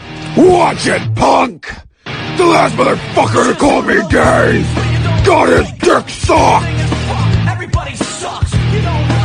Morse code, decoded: none